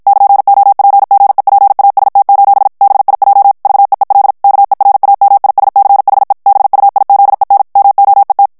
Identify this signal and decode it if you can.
Morse code, heard as a keyed tone, about 56 wpm, 800 Hz. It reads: <KN>OPGEPAST: BIJ VEEL KERAMISCHE BUIZEN MOET
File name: morse mp3 / HOLLANDAIS texte clair 2 vitesse 50 mots minute